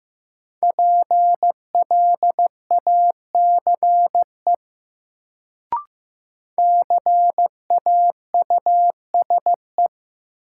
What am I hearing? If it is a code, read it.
Morse code, heard as a keyed tone, about 15 wpm, 700 Hz. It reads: PLACE CAUSE
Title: Sentences from Top 200 Words - New Word Drill - 15wpm